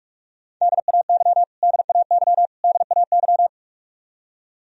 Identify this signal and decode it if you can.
Morse code, heard as a keyed tone, about 46 wpm, 700 Hz. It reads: DAY DAY DAY
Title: Top 100 Words Repeated 45wpm